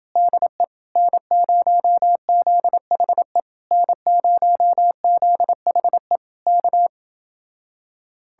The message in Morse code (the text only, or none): DE N0ZHE N0ZHE K